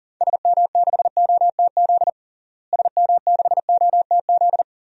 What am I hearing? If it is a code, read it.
Morse code, heard as a keyed tone, about 40 wpm, 700 Hz. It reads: SM6OTZ SM6OTZ
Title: DXCC-100 Call Signs - 6 Characters - Repeated 2x - 40wpm